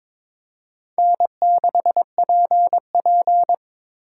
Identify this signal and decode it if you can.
Morse code, heard as a keyed tone, about 22 wpm, 700 Hz. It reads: N6PP
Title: USA Call Signs - 4x Character Spacing - 22wpm